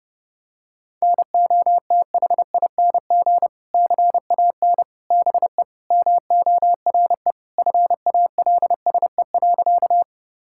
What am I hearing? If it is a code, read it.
Morse code, heard as a keyed tone, about 32 wpm, 700 Hz. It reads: NOTHING CAN BE MORE FALSE.
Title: Sentences from Top 1600 Words 30wpm